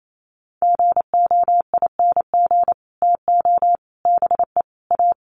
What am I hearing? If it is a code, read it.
Morse code, heard as a keyed tone, about 28 wpm, 700 Hz. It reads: GOING TO BE A
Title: Sets of 4 Words 28wpm